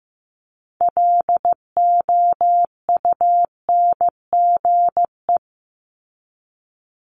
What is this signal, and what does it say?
Morse code, heard as a keyed tone, about 15 wpm, 700 Hz. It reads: LOUNGE
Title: Multi-letter Phonemes 15wpm